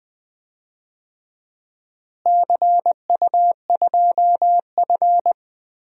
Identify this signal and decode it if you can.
Morse code, heard as a keyed tone, about 21 wpm, 700 Hz. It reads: CU2F
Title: DXCC-100 Call Signs - 4 Characters - Repeated 3x - 20wpm